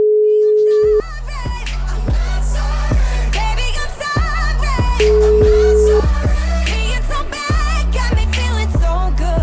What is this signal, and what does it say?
Ringback tone (Europe, 425 Hz), with noise.